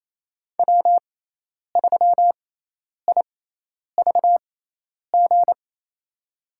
Morse code, heard as a keyed tone, about 28 wpm, spaced as for 10 wpm, 700 Hz. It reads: W3IVG
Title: USA Call Signs - 5 Characters - 6x Character Spacing - 28wpm